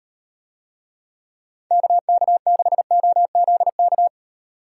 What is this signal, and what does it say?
Morse code, heard as a keyed tone, about 38 wpm, 700 Hz. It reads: KK6OZK